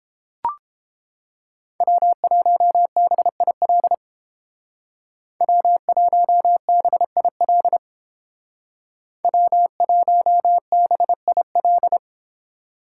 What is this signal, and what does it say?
Morse code, heard as a keyed tone, about 31 wpm, 700 Hz. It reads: W1BIL W1BIL W1BIL